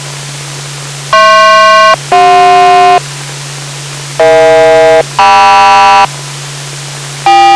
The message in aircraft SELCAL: HP-CJ EH-KM